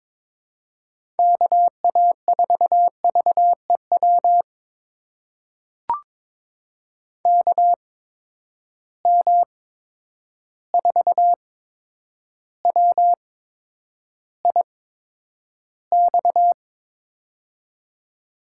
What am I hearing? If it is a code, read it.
Morse code, heard as a keyed tone, about 22 wpm, spaced as for 9 wpm, 700 Hz. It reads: KA4VEW KM4WIX
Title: USA Call Signs - 6 Characters - 8x Character Spacing - 22wpm